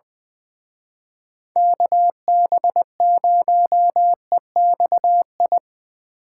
Morse code, heard as a keyed tone, about 23 wpm, 700 Hz. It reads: KB0EXI
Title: Speed Racing - USA 2x3 Call Signs 20wpm